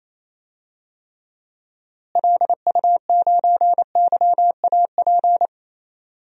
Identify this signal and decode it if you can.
Morse code, heard as a keyed tone, about 27 wpm, 700 Hz. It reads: LU9YAP
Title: DXCC-100 Call Signs - 6 Characters - Repeated 3x - 28wpm